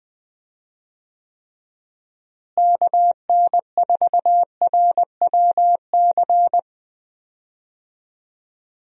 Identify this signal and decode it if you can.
Morse code, heard as a keyed tone, about 20 wpm, 700 Hz. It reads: KN4RWC
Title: USA Call Signs - 6 Characters - Repeated 2x - 20wpm